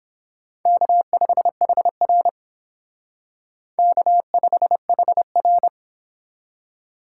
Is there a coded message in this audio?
Yes: Morse code from 0.7 to 5.7 s, heard as a keyed tone, about 29 wpm, 700 Hz. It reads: K5HR K5HR